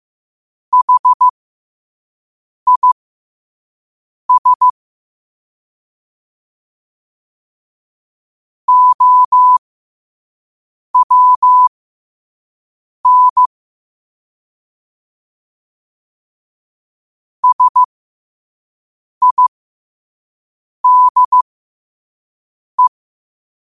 Morse code, heard as a keyed tone, about 15 wpm, spaced as for 5 wpm, 1000 Hz. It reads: HIS OWN SIDE
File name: qotdfiles / 5wpm